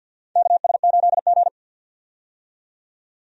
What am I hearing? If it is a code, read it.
Morse code, heard as a keyed tone, about 46 wpm, 700 Hz. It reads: KS8G
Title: USA Call Signs - 4 Characters - 50wpm